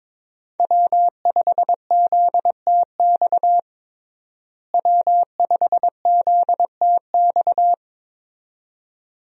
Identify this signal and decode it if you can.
Morse code, heard as a keyed tone, about 22 wpm, 700 Hz. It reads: W5ZTX W5ZTX
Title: USA Call Signs - 5 Characters - Repeated 2x - 22wpm